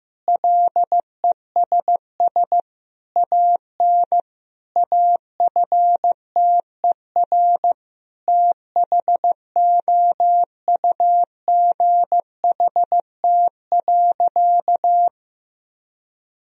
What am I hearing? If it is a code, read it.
Morse code, heard as a keyed tone, about 15 wpm, 700 Hz. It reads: LESS AN AFTER THOUGHT.
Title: Sentences from Top 1000 Words - Review Part 2 - 15wpm